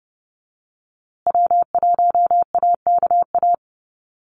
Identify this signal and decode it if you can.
Morse code, heard as a keyed tone, about 30 wpm, 700 Hz. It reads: W1AKA
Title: USA Call Signs Repeated 3x 30wpm